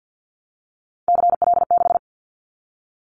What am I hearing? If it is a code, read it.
Morse code, heard as a keyed tone, about 50 wpm, 700 Hz. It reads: CL6
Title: Callsign Prefixes for Top 100 DXCC - 50wpm